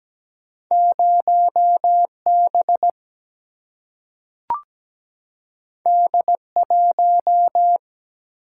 Morse code, heard as a keyed tone, about 17 wpm, 700 Hz. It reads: 0B D1
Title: Single-Letter Single-Number 17wpm